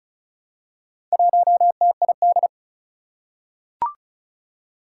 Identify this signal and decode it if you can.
Morse code, heard as a keyed tone, about 36 wpm, 700 Hz. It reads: 1TID
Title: Four Challenging Letter-Number Combinations 35wpm